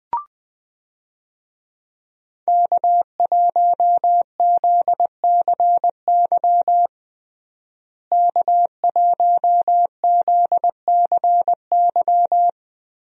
Morse code, heard as a keyed tone, about 20 wpm, 700 Hz. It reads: K1ZCY K1ZCY
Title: USA Call Signs - 5 Characters - Repeated 3x - 20wpm